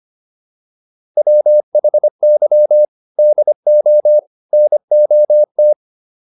Morse code, heard as a keyed tone, about 25 wpm, 600 Hz. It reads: WHY DO NOT
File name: WD205 25